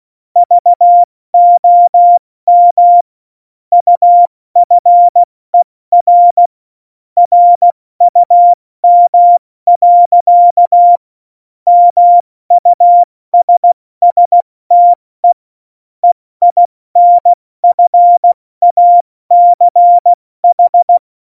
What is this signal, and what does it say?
Morse code, heard as a keyed tone, about 16 wpm, spaced as for 14 wpm, 700 Hz. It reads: VOM UFER RUM. MUSSTE EINFACH
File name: Morsekurs / Witze / Witz 08. Dezember 2020, 70 F80